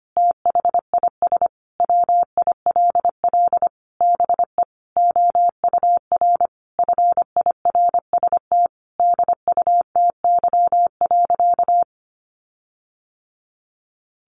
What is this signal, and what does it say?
Morse code, heard as a keyed tone, about 25 wpm, 700 Hz. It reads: THIS WILL BE OUR FIRST DUTY.
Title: Sentences from Top 1500 Words 25wpm